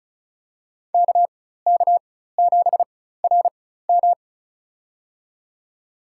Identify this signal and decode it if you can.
Morse code, heard as a keyed tone, about 34 wpm, spaced as for 16 wpm, 700 Hz. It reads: KK7RM
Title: USA Call Signs - 5 Characters - 4x Character Spacing - 35wpm